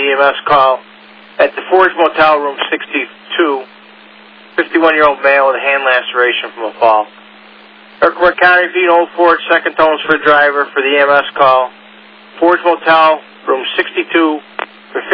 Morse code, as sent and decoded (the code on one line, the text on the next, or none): none